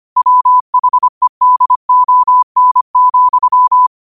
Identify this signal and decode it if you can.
Morse code, heard as a keyed tone, about 25 wpm, 1000 Hz. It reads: WHEDON,
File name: qotdfiles / 25wpm